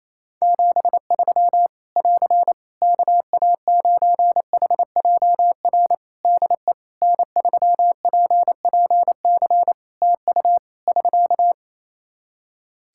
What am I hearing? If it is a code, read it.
Morse code, heard as a keyed tone, about 28 wpm, 700 Hz. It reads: 73 <AR> KA9HJR DE N3PPC TU <SK>
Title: Practice QSO - Exchange 4 - Ending - 28wpm